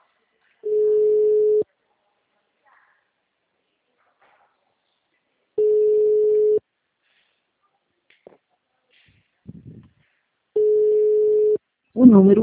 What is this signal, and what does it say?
Ringback tone (Europe, 425 Hz), with speech.